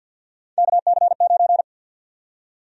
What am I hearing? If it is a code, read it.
Morse code, heard as a keyed tone, about 51 wpm, 700 Hz. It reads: KC9
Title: USA Call Sign Prefixes - Repeated 2x - 50wpm